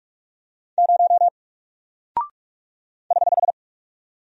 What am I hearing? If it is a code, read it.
Morse code, heard as a keyed tone, about 45 wpm, 700 Hz. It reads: 0 <HH>
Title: Single Character (Letter:Number:Punctuation:Prosign) 45wpm